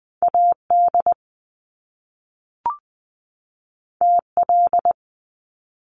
Morse code, heard as a keyed tone, about 19 wpm, 700 Hz. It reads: A D T L
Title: Top Two Letter Combinations in Words 20wpm